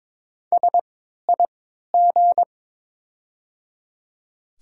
Morse code, heard as a keyed tone, about 22 wpm, 700 Hz. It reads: S I G
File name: Top 90 QSO Elements - 3x Character Spacing - 22wpm